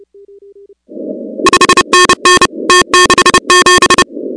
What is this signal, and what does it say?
Morse code, heard as a keyed tone, about 29 wpm, 1080 Hz, with noise. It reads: 5NN T67
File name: contests / 2019-cw-wpx-kt5j-k5pi- -2330 0-16